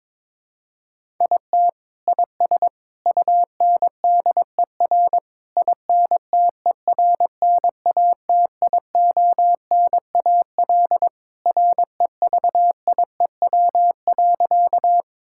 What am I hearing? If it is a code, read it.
Morse code, heard as a keyed tone, about 22 wpm, 700 Hz. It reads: IT IS UNDER INTERNATIONAL REVIEW.
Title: Sentences from Top 1000 Words - Review Part 6 - 22wpm